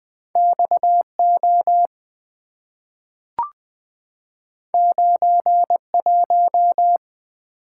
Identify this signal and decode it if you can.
Morse code, heard as a keyed tone, about 20 wpm, 690 Hz. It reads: XO 91
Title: Two Letter-Number Combinations - 20wpm